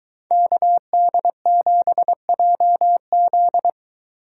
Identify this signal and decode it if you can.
Morse code, heard as a keyed tone, about 22 wpm, 700 Hz. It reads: KD7JZ